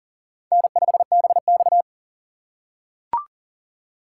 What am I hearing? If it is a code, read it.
Morse code, heard as a keyed tone, about 40 wpm, 700 Hz. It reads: N5BX
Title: USA Call Signs - 4 Characters - 4x Character Spacing - 40wpm